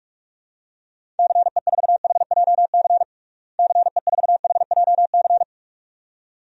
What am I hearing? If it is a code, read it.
Morse code, heard as a keyed tone, about 41 wpm, 700 Hz. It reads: CE4HJC CE4HJC
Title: DXCC-100 Call Signs - 6 Characters - Repeated 2x - 45wpm